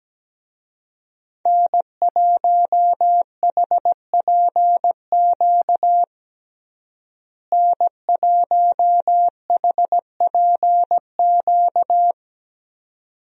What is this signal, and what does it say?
Morse code, heard as a keyed tone, about 17 wpm, 700 Hz. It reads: N1HPQ N1HPQ
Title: USA Call Signs - 5 Characters - Repeated 2x - 17wpm